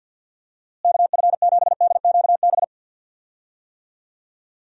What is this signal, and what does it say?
Morse code, heard as a keyed tone, about 46 wpm, 700 Hz. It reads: KF7DQB
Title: USA Call Signs - 6 Characters - Repeated 2x - 50wpm